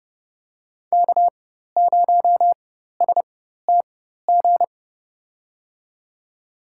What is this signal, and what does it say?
Morse code, heard as a keyed tone, about 32 wpm, spaced as for 14 wpm, 700 Hz. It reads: K0STG
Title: USA Call Signs - 5 Characters - 4x Character Spacing - 30wpm